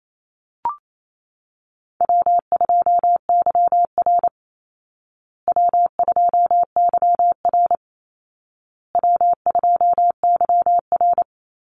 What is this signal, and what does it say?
Morse code, heard as a keyed tone, about 28 wpm, 700 Hz. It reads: W2YR W2YR W2YR
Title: Call Signs Repeated 2x - 28wpm